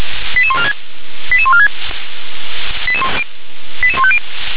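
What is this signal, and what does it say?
Five-tone selective call (ZVEI), heard as quick runs of five tones, tones of about 70 ms. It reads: 88146 88146 88150
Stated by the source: Five-tone ZVEI selective call received on SDR.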